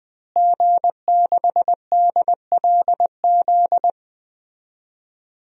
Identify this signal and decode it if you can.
Morse code, heard as a keyed tone, about 20 wpm, 700 Hz. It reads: G6DLZ
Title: DXCC-100 Call Signs - 5 Characters - Repeated 3x - 20wpm